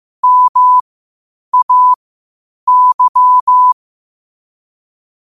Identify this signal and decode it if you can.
Morse code, heard as a keyed tone, about 15 wpm, spaced as for 8 wpm, 1000 Hz. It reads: MAY
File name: qotdfiles / 8wpm